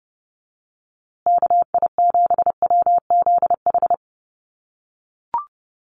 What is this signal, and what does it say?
Morse code, heard as a keyed tone, about 30 wpm, 700 Hz. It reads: KI7WZH
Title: USA Call Signs Repeated 2x 30wpm